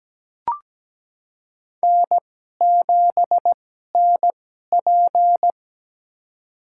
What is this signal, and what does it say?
Morse code, heard as a keyed tone, about 17 wpm, spaced as for 12 wpm, 700 Hz. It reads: N7NP
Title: USA Call Signs - 4 Characters - 2x Character Spacing - 17wpm